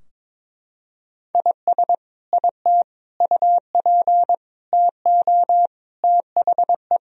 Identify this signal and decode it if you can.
Morse code, heard as a keyed tone, about 22 wpm, 700 Hz. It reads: IS IT UP TO THE
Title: Sentences from Top 1000 Words - Review Part 4 - 22wpm